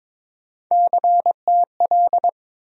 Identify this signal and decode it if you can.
Morse code, heard as a keyed tone, about 22 wpm, 700 Hz. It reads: CTL
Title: USA Call Sign Postfixes - 3 Letters 22wpm